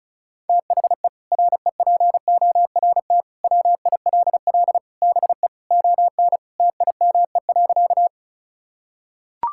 Morse code, heard as a keyed tone, about 36 wpm, 700 Hz. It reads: THE REPORT WILL BE ON TIME.
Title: Sentences from Top 1000 Words - Review Part 6 - 35wpm